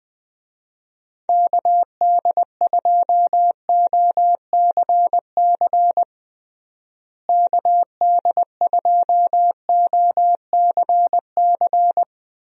Morse code, heard as a keyed tone, about 20 wpm, 700 Hz. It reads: KD2OCC KD2OCC
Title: USA Call Signs - 6 Characters - Repeated 2x - 20wpm